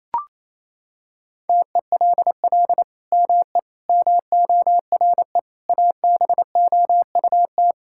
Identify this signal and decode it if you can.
Morse code, heard as a keyed tone, about 27 wpm, 700 Hz. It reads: TELL ME MORE ABOUT
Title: Sentences from Top 1000 Words - Review Part 5 - 28wpm